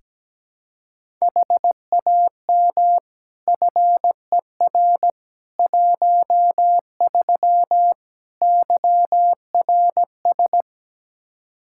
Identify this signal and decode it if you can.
Morse code, heard as a keyed tone, about 17 wpm, 700 Hz. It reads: HAM FER 13 YRS
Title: Practice QSO - Exchange 3 - More Info - 17wpm